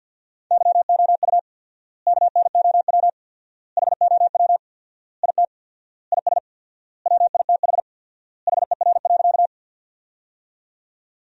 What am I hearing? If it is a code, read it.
Morse code, heard as a keyed tone, about 52 wpm, 700 Hz. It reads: YOU KNOW HOW IT IS WITH HER.
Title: 4x Word-Spacing - Sentences from Top 100 Words - 50wpm